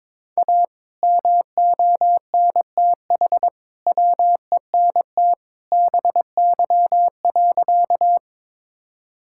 Morse code, heard as a keyed tone, about 22 wpm, 700 Hz. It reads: A MONTH WENT BY.
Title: Sentences from Top 1000 Words - Review Part 5 - 22wpm